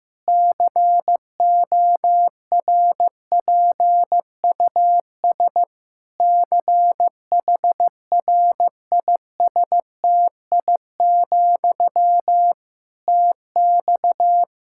Morse code, heard as a keyed tone, about 15 wpm, 700 Hz. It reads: CORPUS CHRISTI, TX
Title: K1USN SST QSO Practice 15wpm